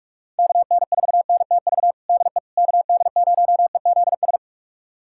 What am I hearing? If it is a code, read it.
Morse code, heard as a keyed tone, about 45 wpm, 700 Hz. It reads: KN4NTV DE KD0EZS
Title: USA Callsign DE Callsign 45wpm